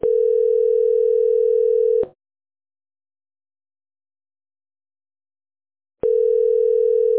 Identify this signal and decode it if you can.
Ringback tone (North America), 440 + 480 Hz.